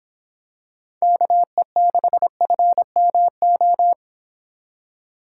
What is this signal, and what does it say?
Morse code, heard as a keyed tone, about 26 wpm, 700 Hz. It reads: KE6FMO